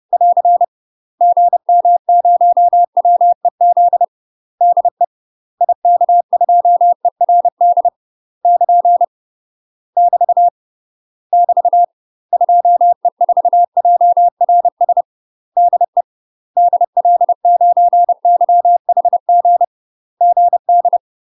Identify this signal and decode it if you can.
Morse code, heard as a keyed tone, about 30 wpm, 700 Hz. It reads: <AR> GM0WEZ DE IK2ERD <KN> <BT> <BT> 2E4JRS DE DL9YHG GD
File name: UK MASTER QSOs TRAINING / 30WPM / UK MASTER QSOs 0048